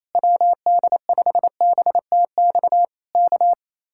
Morse code, heard as a keyed tone, about 27 wpm, 700 Hz. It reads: WD5BTX K
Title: Speed-Racing Common QSO Phrases - 28wpm